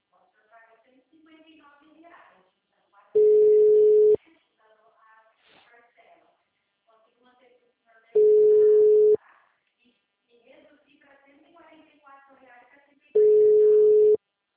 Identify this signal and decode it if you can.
Ringback tone (Europe, 425 Hz), with speech.